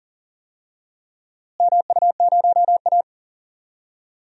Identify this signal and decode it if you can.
Morse code, heard as a keyed tone, about 40 wpm, 700 Hz. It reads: MU0A
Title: DXCC-100 Call Signs - 4 Characters - Repeated 3x - 40wpm